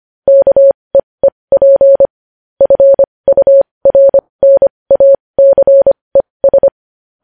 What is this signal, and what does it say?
Morse code, heard as a keyed tone, about 25 wpm, spaced as for 20 wpm, 550 Hz. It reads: KEEP FURNACES
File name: podcast / News headlines-1x-25x20